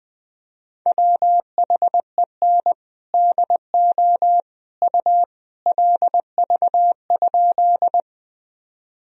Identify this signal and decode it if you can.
Morse code, heard as a keyed tone, about 21 wpm, 700 Hz. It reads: WHEN DO U LV?
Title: Sentences from Top 400 Words and Abbreviations 20wpm